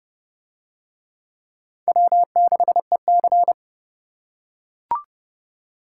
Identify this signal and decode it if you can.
Morse code, heard as a keyed tone, about 30 wpm, 700 Hz. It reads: W6EC